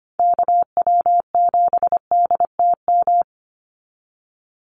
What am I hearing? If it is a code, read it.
Morse code, heard as a keyed tone, about 25 wpm, 700 Hz. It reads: KW7DTM
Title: USA Call Signs 25wpm